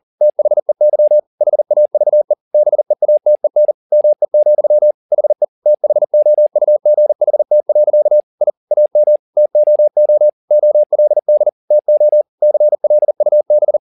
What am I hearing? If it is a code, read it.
Morse code, heard as a keyed tone, about 40 wpm, 600 Hz. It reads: THEY HAVE BEATEN ME, HE THOUGHT. I AM TOO OLD TO CLUB
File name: the old man and the sea 2 40wpm